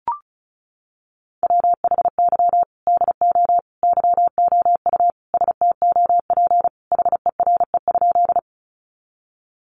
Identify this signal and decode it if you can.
Morse code, heard as a keyed tone, about 34 wpm, 700 Hz. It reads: WHY DO YOU STOP HERE?
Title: Sentences from Top 300 Words 35wpm